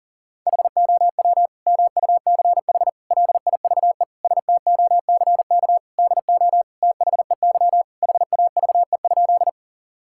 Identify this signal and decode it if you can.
Morse code, heard as a keyed tone, about 40 wpm, 700 Hz. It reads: HOW MUCH LIVE STOCK DO THEY HAVE?
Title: Sentences from Top 1000 Words - Review Part 5 - 40wpm